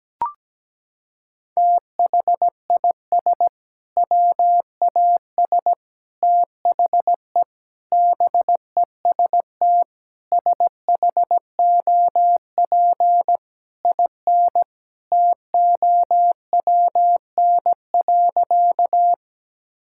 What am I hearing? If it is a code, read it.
Morse code, heard as a keyed tone, about 17 wpm, 700 Hz. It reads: THIS WAS THE BEST SHOP IN TOWN.
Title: Sentences from Top 1600 Words 17wpm